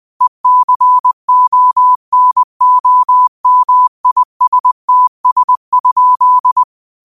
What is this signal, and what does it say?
Morse code, heard as a keyed tone, about 20 wpm, 1000 Hz. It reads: ECONOMISTS?
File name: qotdfiles / 20wpm